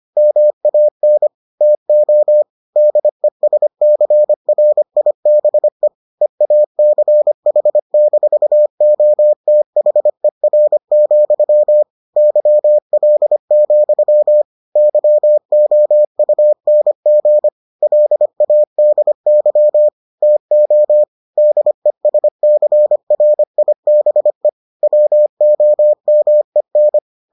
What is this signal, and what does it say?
Morse code, heard as a keyed tone, about 25 wpm, 600 Hz. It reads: MAN TO DESCRIBE EACH-OTHER, YL, YOUNG LADY TO DESCRIBE WOMEN